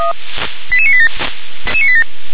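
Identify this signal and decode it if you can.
Five-tone selective call (ZVEI), heard as quick runs of five tones, tones of about 70 ms. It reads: 88987
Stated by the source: ZVEI five-tone selective call received by SDR.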